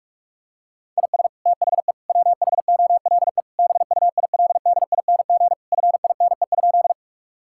Morse code, heard as a keyed tone, about 46 wpm, 700 Hz. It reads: IS THE WHOLE BUILDING FINE?
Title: Sentences from Top 1000 Words - Review Part 5 - 45wpm